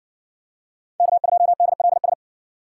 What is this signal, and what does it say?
Morse code, heard as a keyed tone, about 60 wpm, 700 Hz. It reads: K2DLS